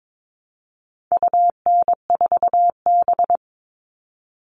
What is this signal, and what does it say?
Morse code, heard as a keyed tone, about 22 wpm, 700 Hz. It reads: UN4B
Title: All Cut Numbers - 4-Digit 22wpm